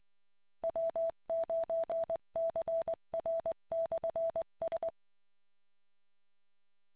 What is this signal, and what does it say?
Morse code, heard as a keyed tone, about 20 wpm, 660 Hz. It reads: W9CR/S